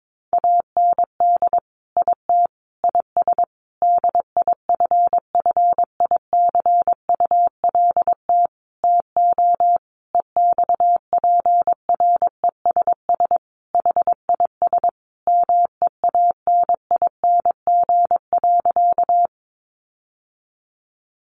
Morse code, heard as a keyed tone, about 22 wpm, 700 Hz. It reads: AND IT IS DIFFICULT TO EXPRESS HIS MEANING.
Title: Sentences from Top 1100 Words 22wpm